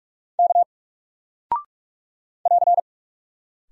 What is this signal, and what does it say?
Morse code, heard as a keyed tone, about 45 wpm, 700 Hz. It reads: K <AR>
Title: Single Character (Letter:Number:Punctuation:Prosign) Alt 45wpm